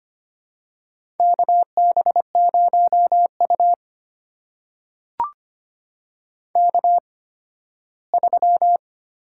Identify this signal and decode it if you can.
Morse code, heard as a keyed tone, about 25 wpm, spaced as for 15 wpm, 700 Hz. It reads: KB0U K3
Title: USA Call Signs - 8x Character Spacing - 25wpm